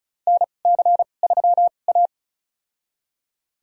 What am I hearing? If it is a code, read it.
Morse code, heard as a keyed tone, about 36 wpm, spaced as for 25 wpm, 690 Hz. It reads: NC3A
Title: USA Call Signs - 4 Characters - 2x Character Spacing - 35wpm